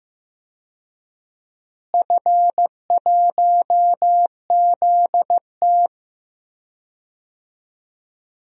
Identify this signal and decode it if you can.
Morse code, heard as a keyed tone, about 15 wpm, 700 Hz. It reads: F1ZT
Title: DXCC-100 Call Signs - 4 Characters - Repeated 2x - 15wpm